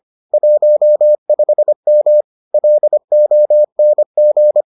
Morse code, heard as a keyed tone, about 25 wpm, 600 Hz. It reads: 15M LONG